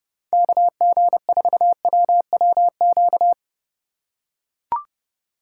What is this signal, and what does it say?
Morse code, heard as a keyed tone, about 32 wpm, 700 Hz. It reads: KG4WWQ
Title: USA Call Signs - 6 Characters - 4x Character Spacing - 30wpm